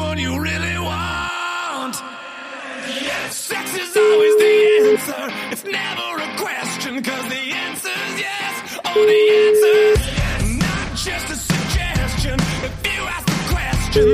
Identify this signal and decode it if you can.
Ringback tone (Europe, 425 Hz), with speech.